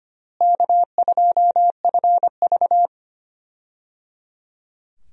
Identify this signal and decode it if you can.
Morse code, heard as a keyed tone, about 25 wpm, 700 Hz. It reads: K2FV
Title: USA Call Signs - 4 Characters - Repeated 3x - 25wpm